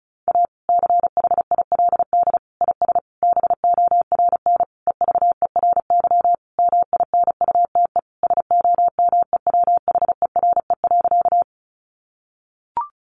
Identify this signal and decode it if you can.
Morse code, heard as a keyed tone, about 34 wpm, 700 Hz. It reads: A CHILD IS BORN EVERY MINUTE SOMEWHERE.
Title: Sentences from Top 1400 Words 35wpm